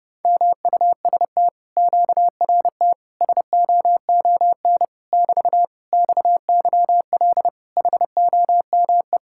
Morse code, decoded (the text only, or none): MUST QRT SOON <BT> XYL HOME